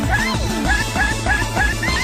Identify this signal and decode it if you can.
There is no code.